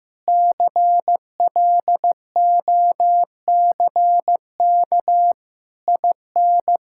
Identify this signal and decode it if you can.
Morse code, heard as a keyed tone, about 15 wpm, 700 Hz. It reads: CLOCK IN
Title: Pangram Challenge v2 - 15wpm